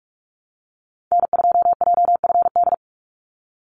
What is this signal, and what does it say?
Morse code, heard as a keyed tone, about 44 wpm, 700 Hz. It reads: N2JFD
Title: Call Signs Repeated 3x - 45wpm